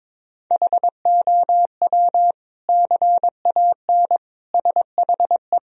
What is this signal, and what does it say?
Morse code, heard as a keyed tone, about 22 wpm, 700 Hz. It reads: HOW CAN SHE
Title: Sentences from Top 1000 Words - Review Part 4 - 22wpm